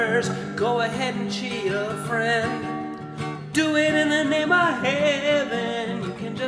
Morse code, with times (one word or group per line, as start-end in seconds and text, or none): none